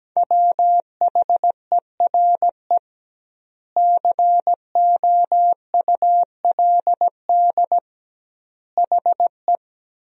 Morse code, heard as a keyed tone, about 17 wpm, 700 Hz. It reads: WHERE COULD HE
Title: 2x Word-Spacing - Sentences from Top 200 Words - 17wpm